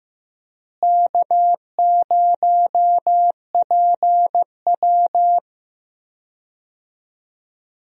Morse code, heard as a keyed tone, about 15 wpm, 700 Hz. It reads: K0PW
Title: USA Call Signs - 4 Characters - Repeated 3x - 15wpm